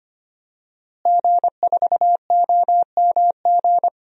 Morse code, heard as a keyed tone, about 25 wpm, 700 Hz. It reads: G4OMG
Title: DXCC-100 Call Signs - 5 Characters - Repeated 3x - 25wpm